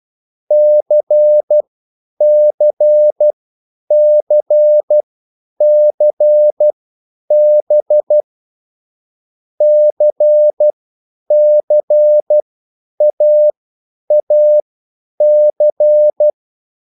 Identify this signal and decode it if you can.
Morse code, heard as a keyed tone, about 12 wpm, spaced as for 9 wpm, 600 Hz. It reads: CCCCB CCAAC